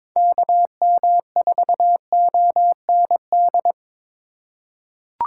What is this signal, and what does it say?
Morse code, heard as a keyed tone, about 22 wpm, 700 Hz. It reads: KM4OND
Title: USA Call Signs - 6 Characters - 3x Character Spacing - 22wpm